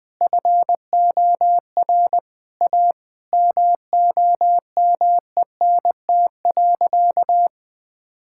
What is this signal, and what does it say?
Morse code, heard as a keyed tone, about 20 wpm, 700 Hz. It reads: FOR A MOMENT.
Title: Sentences with 3-Word Phrases - 20wpm